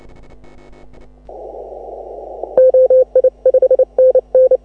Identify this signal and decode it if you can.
Morse code, heard as a keyed tone, about 30 wpm, 530 Hz. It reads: OI 5NN